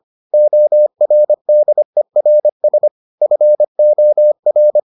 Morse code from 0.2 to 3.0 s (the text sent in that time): ORDERS